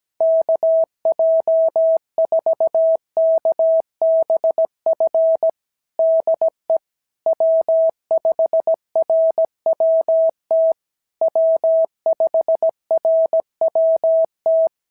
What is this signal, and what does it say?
Morse code, heard as a keyed tone, about 15 wpm, 650 Hz. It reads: KJ4KBF DE W5RWT W5RWT
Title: SOTA QSO Practice 17wpm